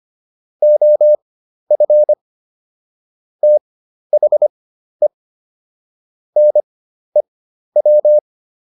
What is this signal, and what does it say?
Morse code, heard as a keyed tone, about 25 wpm, spaced as for 12 wpm, 600 Hz. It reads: OF THE NEW